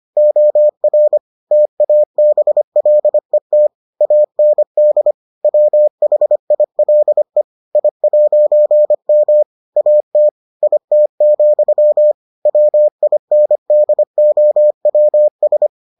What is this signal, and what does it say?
Morse code, heard as a keyed tone, about 25 wpm, 600 Hz. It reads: OR TABLET AND WHILE I'M AT IT, WINDOWS